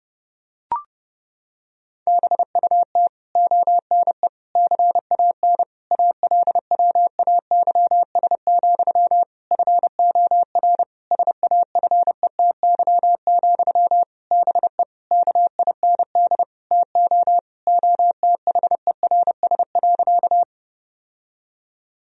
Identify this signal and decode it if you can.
Morse code, heard as a keyed tone, about 32 wpm, 700 Hz. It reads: BUT ONE CAN ALWAYS, FOR SAFETY, BE KIND TO OTHERS.
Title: Sentences from Top 1000 Words - Review Part 4 - 30wpm